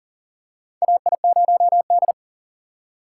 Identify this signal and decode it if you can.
Morse code, heard as a keyed tone, about 40 wpm, 700 Hz. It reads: AI0D
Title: USA Call Signs - 4 Characters - 4x Character Spacing - 40wpm